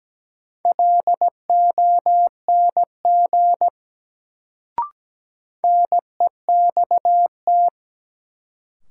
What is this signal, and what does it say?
Morse code, heard as a keyed tone, about 17 wpm, 700 Hz. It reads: LONG NEXT
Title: Sets of 2 Words - New Word Drill - 17wpm